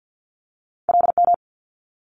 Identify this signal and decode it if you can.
Morse code, heard as a keyed tone, about 50 wpm, 700 Hz. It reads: LM
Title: Multi-letter Phonemes Quick Fire 50wpm